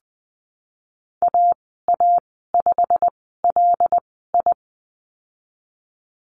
Morse code, heard as a keyed tone, about 20 wpm, spaced as for 15 wpm, 700 Hz. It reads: AA5LI
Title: 161 Lesson 26g - Letter g - Call Signs 20wpm